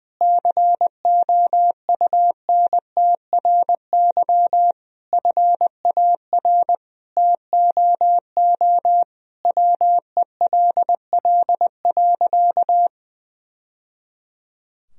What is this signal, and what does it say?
Morse code, heard as a keyed tone, about 20 wpm, 700 Hz. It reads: COUNTRY FAR TOO WELL.
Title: Sentences from Top 1000 Words - Review Part 3 - 20wpm